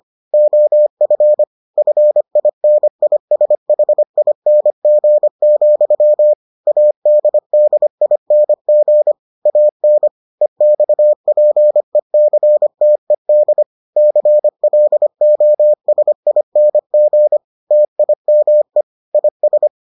Morse code, heard as a keyed tone, about 25 wpm, 600 Hz. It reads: OF FINISHING, ADDING AN EXPECTED CLOSING TIME IS